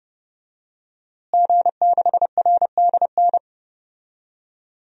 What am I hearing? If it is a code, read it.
Morse code, heard as a keyed tone, about 29 wpm, 700 Hz. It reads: G6RDN
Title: DXCC-100 Call Signs - 5 Characters - Repeated 2x - 30wpm